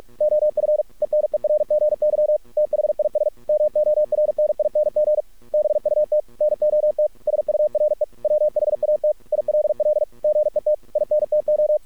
Morse code, heard as a keyed tone, about 46 wpm, 600 Hz. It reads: OF ENERGY THIS MORNING BUT NOT SURE WHAT ILL GET INTO
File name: archived / CRSnet